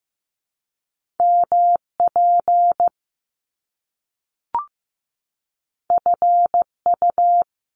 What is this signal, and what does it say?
Morse code, heard as a keyed tone, about 15 wpm, 700 Hz. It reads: MP FU
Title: Top Two Letter Combinations in Words 15wpm